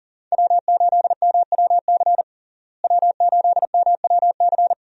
Morse code, heard as a keyed tone, about 40 wpm, 700 Hz. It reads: W8MWC W8MWC
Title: USA Call Signs - 5 Characters - Repeated 2x - 40wpm